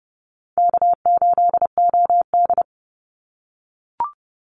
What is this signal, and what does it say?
Morse code, heard as a keyed tone, about 30 wpm, 700 Hz. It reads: K8OD
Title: Easy Call Signs - 30wpm